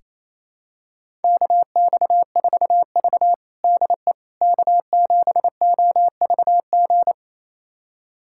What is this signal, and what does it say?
Morse code, heard as a keyed tone, about 28 wpm, 700 Hz. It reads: KX4V DE K7OVG
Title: USA Callsign DE Callsign 28wpm